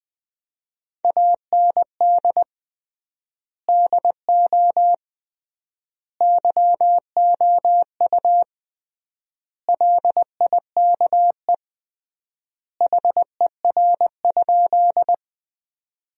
Morse code, heard as a keyed tone, about 20 wpm, 700 Hz. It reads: AND DO YOU LIKE HER?
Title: 3x Word-Spacing - Sentences from Top 100 Words - 20wpm